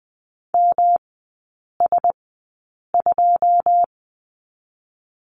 Morse code, heard as a keyed tone, about 20 wpm, 700 Hz. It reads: M S 2
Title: 107 Lesson 18e - Letter m - Three Characters 20wpm